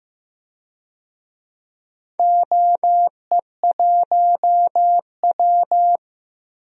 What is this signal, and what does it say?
Morse code, heard as a keyed tone, about 15 wpm, 700 Hz. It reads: OE1W
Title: DXCC-100 Call Signs - 4 Characters - Repeated 2x - 15wpm